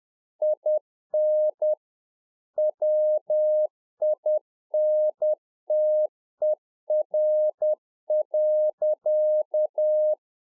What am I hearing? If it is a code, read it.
Morse code, heard as a keyed tone, about 10 wpm, 610 Hz. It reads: IN WINTER.